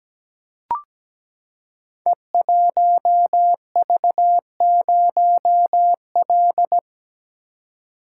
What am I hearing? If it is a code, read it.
Morse code, heard as a keyed tone, about 17 wpm, 700 Hz. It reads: E1V0L
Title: Code Groups - 5 Letter:Numbers - 17wpm